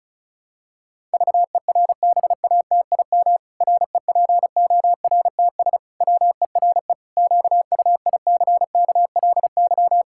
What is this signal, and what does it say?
Morse code, heard as a keyed tone, about 34 wpm, 700 Hz. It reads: VERBATIM REPORTS WERE QUICKLY
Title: Pangram Challenge v2 - 35wpm